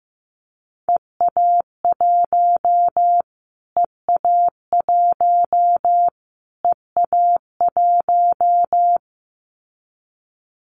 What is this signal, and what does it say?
Morse code, heard as a keyed tone, about 15 wpm, 700 Hz. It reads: EA1 EA1 EA1
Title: Callsign Prefixes for Top 25 DXCC - Repeated 3x - 15wpm